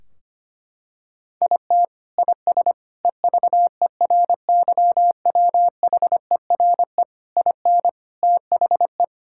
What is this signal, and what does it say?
Morse code, heard as a keyed tone, about 25 wpm, 700 Hz. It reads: IT IS EVERYWHERE IN THE
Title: Sentences from Top 1600 Words 25wpm